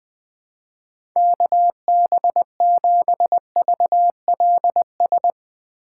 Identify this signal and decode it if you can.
Morse code, heard as a keyed tone, about 21 wpm, 700 Hz. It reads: KB7VLS
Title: USA Call Signs - 6 Characters - 2x Character Spacing - 20wpm